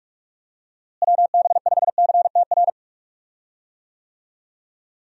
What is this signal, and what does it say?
Morse code, heard as a keyed tone, about 46 wpm, 700 Hz. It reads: WB5CTR
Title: USA Call Signs - 6 Characters - Repeated 2x - 45wpm